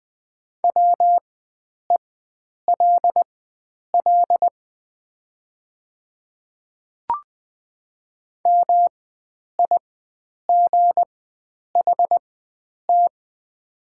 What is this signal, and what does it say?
Morse code, heard as a keyed tone, about 20 wpm, spaced as for 9 wpm, 700 Hz. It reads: WELL MIGHT
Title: 4x Character Spacing - Top 300 Words - 20wpm